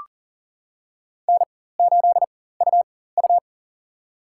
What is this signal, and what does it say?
Morse code, heard as a keyed tone, about 40 wpm, spaced as for 19 wpm, 700 Hz. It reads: N8UU
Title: USA Call Signs - 4 Characters - 4x Character Spacing - 40wpm